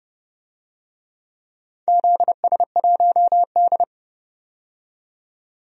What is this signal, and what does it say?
Morse code, heard as a keyed tone, about 30 wpm, 700 Hz. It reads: ZS1D
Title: DXCC-100 Call Signs - 4 Characters - 30wpm